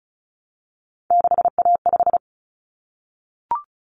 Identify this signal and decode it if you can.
Morse code, heard as a keyed tone, about 34 wpm, 700 Hz. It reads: 6A5
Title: Common Cut Numbers - 3-Digit 35wpm